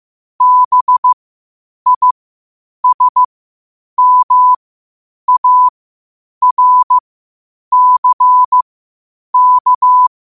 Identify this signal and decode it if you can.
Morse code, heard as a keyed tone, about 15 wpm, spaced as for 8 wpm, 1000 Hz. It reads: BISMARCK
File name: qotdfiles / 8wpm